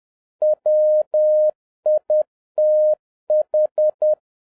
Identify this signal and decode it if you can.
Morse code, heard as a keyed tone, about 10 wpm, 610 Hz. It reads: WITH
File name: 2011-10-mp3 / n11-MEETING WITH GOOD FRIENDS 10wpm